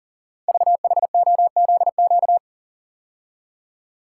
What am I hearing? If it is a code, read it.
Morse code, heard as a keyed tone, about 40 wpm, 700 Hz. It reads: VHOZQ
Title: Code Groups - 5 Letters - 40wpm